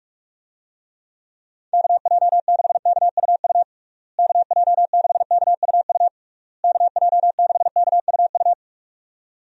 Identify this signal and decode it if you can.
Morse code, heard as a keyed tone, about 46 wpm, 700 Hz. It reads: KJ6KUU KJ6KUU KJ6KUU
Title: USA Call Signs - 6 Characters - Repeated 3x - 45wpm